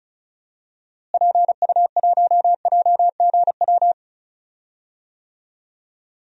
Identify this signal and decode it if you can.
Morse code, heard as a keyed tone, about 36 wpm, 700 Hz. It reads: PU1JGW
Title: DXCC-100 Call Signs - 6 Characters - Repeated 2x - 35wpm